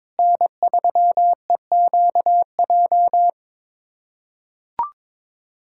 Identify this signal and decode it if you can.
Morse code, heard as a keyed tone, about 22 wpm, 700 Hz. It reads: N3EQJ
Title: USA Callsign DE Callsign - 5 Characters - 22wpm